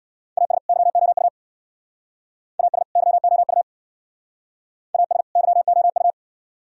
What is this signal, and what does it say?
Morse code, heard as a keyed tone, about 62 wpm, 700 Hz. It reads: AS YOU AS YOU AS YOU
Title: Speed-Racing - Sets of 2 Words 50wpm